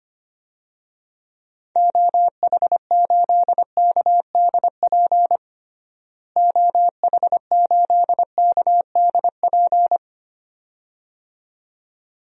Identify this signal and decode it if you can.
Morse code, heard as a keyed tone, about 25 wpm, 700 Hz. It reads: OH8KDP OH8KDP
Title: DXCC-100 Call Signs - 6 Characters - Repeated 2x - 25wpm